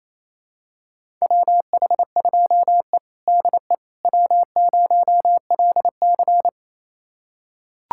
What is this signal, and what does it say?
Morse code, heard as a keyed tone, about 28 wpm, 700 Hz. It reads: WH2E DE W0LC
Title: USA Callsign DE Callsign - 4 Characters - 28wpm